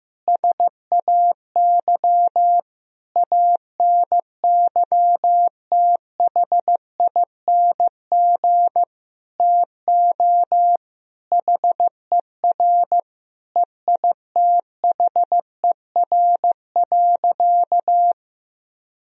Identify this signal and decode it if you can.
Morse code, heard as a keyed tone, about 16 wpm, 700 Hz. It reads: SAY ANYTHING TO HER EITHER.
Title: Sentences from Top 1000 Words - Review Part 4 - 15wpm